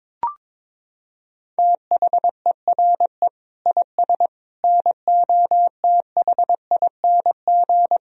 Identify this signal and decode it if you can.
Morse code, heard as a keyed tone, about 22 wpm, 700 Hz. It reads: THERE IS NOTHING
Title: Sentences from Top 1000 Words - Review Part 1 - 22wpm